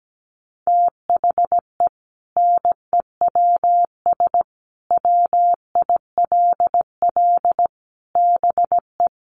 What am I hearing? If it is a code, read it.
Morse code, heard as a keyed tone, about 17 wpm, 700 Hz. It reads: THE NEWS WILL BE
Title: Sentences from Top 1200 Words 17wpm